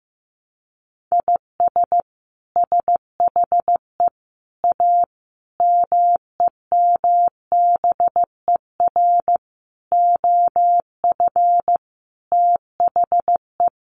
Morse code, heard as a keyed tone, about 15 wpm, 700 Hz. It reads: IS SHE A MEMBER OF THE
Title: Sentences from Top 1100 Words 15wpm